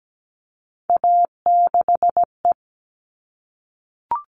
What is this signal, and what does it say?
Morse code, heard as a keyed tone, about 17 wpm, 700 Hz. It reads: A6E
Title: All Cut Numbers - 3-Digit 17wpm